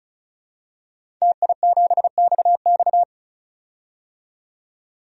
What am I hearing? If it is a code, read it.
Morse code, heard as a keyed tone, about 36 wpm, 700 Hz. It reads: TI7XX
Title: DXCC-100 Call Signs - 5 Characters - 35wpm